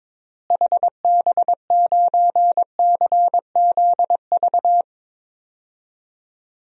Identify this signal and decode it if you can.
Morse code, heard as a keyed tone, about 22 wpm, 700 Hz. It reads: HB9CZV
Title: DXCC-100 Call Signs - 6 Characters - Repeated 3x - 22wpm